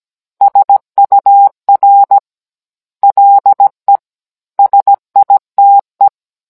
Morse code, heard as a keyed tone, about 17 wpm, 800 Hz. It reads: SUR LE SITE
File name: Cours-NR19-16mm